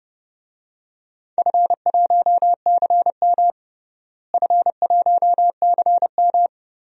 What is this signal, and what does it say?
Morse code, heard as a keyed tone, about 31 wpm, 700 Hz. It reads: F1CM F1CM
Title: DXCC-100 Call Signs - 4 Characters - Repeated 2x - 30wpm